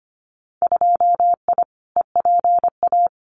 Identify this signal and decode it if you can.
Morse code, heard as a keyed tone, about 25 wpm, 700 Hz. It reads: 2I EPA
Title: Winter Field Day Exchanges Encore 25wpm